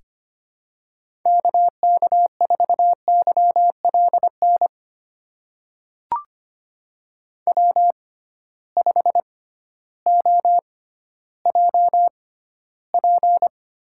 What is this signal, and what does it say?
Morse code, heard as a keyed tone, about 25 wpm, spaced as for 14 wpm, 700 Hz. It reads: KK4YLN W5OJP
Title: USA Call Signs - 6x Character Spacing - 25wpm